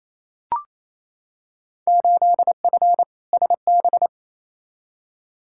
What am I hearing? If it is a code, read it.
Morse code, heard as a keyed tone, about 28 wpm, 700 Hz. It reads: 8F SB
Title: Field Day Exchanges Encore - 28wpm